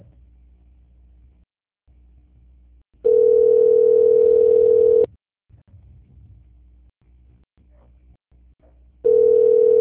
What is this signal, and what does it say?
Ringback tone (North America), 440 + 480 Hz.
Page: Bangu - RJ